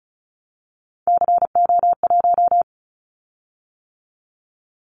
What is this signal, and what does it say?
Morse code, heard as a keyed tone, about 34 wpm, 700 Hz. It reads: CO1
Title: Callsign Prefixes for Top 100 DXCC - 35wpm